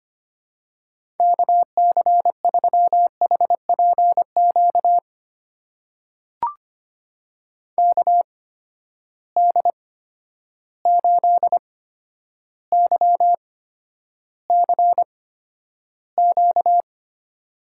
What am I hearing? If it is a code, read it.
Morse code, heard as a keyed tone, about 25 wpm, spaced as for 11 wpm, 700 Hz. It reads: KC3HPQ KD8YCQ
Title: USA Call Signs - 6 Characters - 8x Character Spacing - 25wpm